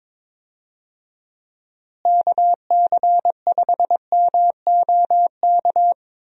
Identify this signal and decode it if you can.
Morse code, heard as a keyed tone, about 22 wpm, 700 Hz. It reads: KC5MOK
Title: USA Call Signs - 6 Characters - Repeated 3x - 22wpm